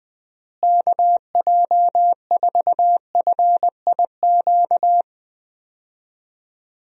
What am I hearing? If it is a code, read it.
Morse code, heard as a keyed tone, about 20 wpm, 700 Hz. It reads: KJ4FIQ
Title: USA Call Signs - 6 Characters - Repeated 3x - 20wpm